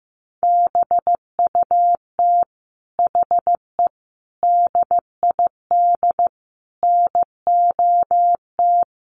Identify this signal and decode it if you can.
Morse code, heard as a keyed tone, about 15 wpm, 700 Hz. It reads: BUT HE DID NOT
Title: Sentences from Top 1500 Words 15wpm